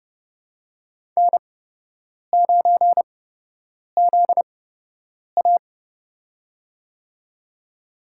Morse code, heard as a keyed tone, about 31 wpm, spaced as for 8 wpm, 700 Hz. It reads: N9ZA
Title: USA Call Signs - 4 Characters - 8x Character Spacing - 30wpm